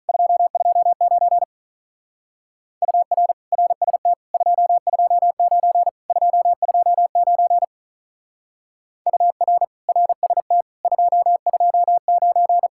Morse code, heard as a keyed tone, about 43 wpm, 700 Hz. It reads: 229 UR RST 229 229 UR RST 229
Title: Speed-Racing Common QSO Phrases - 35wpm